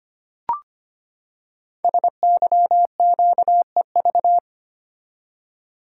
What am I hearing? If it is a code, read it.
Morse code, heard as a keyed tone, about 25 wpm, 700 Hz. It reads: SYQEV
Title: Code Groups - 5 Letters - 25wpm